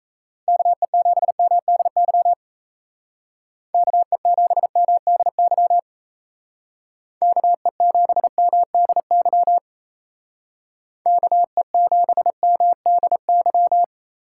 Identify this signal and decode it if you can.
Morse code, heard as a keyed tone, about 33 wpm, 700 Hz. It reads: KE7MDY KE7MDY KE7MDY KE7MDY
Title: Speed Racing - USA 2x3 Call Signs 28wpm